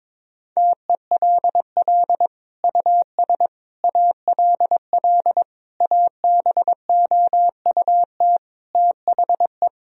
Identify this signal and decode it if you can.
Morse code, heard as a keyed tone, about 21 wpm, 700 Hz. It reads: TELL US ALL ABOUT THE
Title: Sentences from Top 1000 Words - Review Part 4 - 22wpm